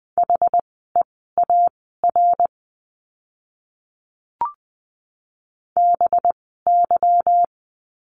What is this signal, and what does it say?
Morse code, heard as a keyed tone, about 20 wpm, spaced as for 14 wpm, 700 Hz. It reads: HEAR BY
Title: 244 Lesson 39f - Letter z - Words 20wpm